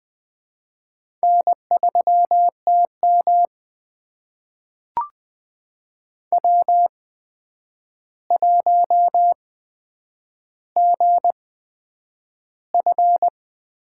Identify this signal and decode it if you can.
Morse code, heard as a keyed tone, about 20 wpm, spaced as for 9 wpm, 700 Hz. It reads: N3TM W1GF
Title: USA Call Signs - 4 Characters - 8x Character Spacing - 20wpm